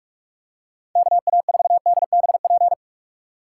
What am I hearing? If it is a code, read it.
Morse code, heard as a keyed tone, about 46 wpm, 700 Hz. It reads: KA4DBP
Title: USA Call Signs - 6 Characters - 4x Character Spacing - 45wpm